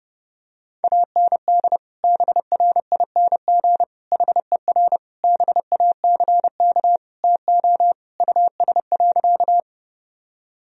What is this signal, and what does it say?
Morse code, heard as a keyed tone, about 32 wpm, 700 Hz. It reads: AND BRING HER BACK TO US.
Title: Sentences from Top 1000 Words - Review Part 3 - 30wpm